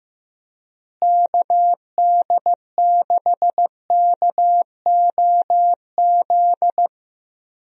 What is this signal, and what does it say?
Morse code, heard as a keyed tone, about 15 wpm, 700 Hz. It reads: KD6KOZ
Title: USA Call Signs - 6 Characters - Repeated 2x - 15wpm